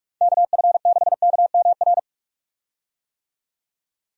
Morse code, heard as a keyed tone, about 43 wpm, 700 Hz. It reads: KF6KMR
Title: USA Call Signs - 6 Characters - 45wpm